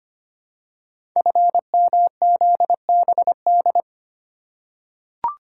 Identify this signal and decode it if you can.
Morse code, heard as a keyed tone, about 25 wpm, 700 Hz. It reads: FMZBD
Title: Code Groups - 5 Letters - 25wpm